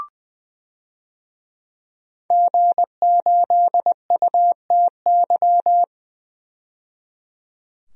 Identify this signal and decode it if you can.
Morse code, heard as a keyed tone, about 20 wpm, 700 Hz. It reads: G8UTY
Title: DXCC-100 Call Signs - 5 Characters - 20wpm